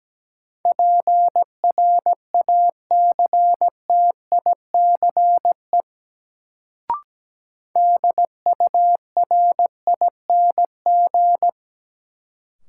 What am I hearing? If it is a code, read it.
Morse code, heard as a keyed tone, about 17 wpm, 700 Hz. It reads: PRACTICE DURING
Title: Sentences from Top 700 Words - New Word Drill - 17wpm